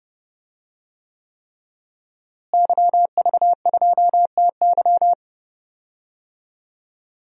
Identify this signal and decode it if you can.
Morse code, heard as a keyed tone, about 30 wpm, 700 Hz. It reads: YV2TY
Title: DXCC-100 Call Signs - 5 Characters - Repeated 3x - 30wpm